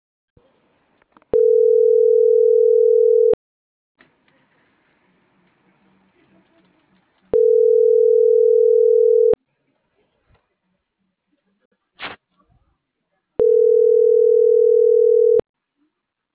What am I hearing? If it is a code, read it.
Ringback tone (North America), 440 + 480 Hz.